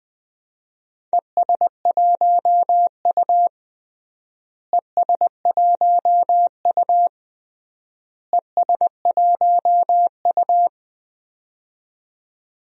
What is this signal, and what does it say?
Morse code, heard as a keyed tone, about 20 wpm, 700 Hz. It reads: ES1U ES1U ES1U
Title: DXCC-100 Call Signs - 4 Characters - Repeated 3x - 20wpm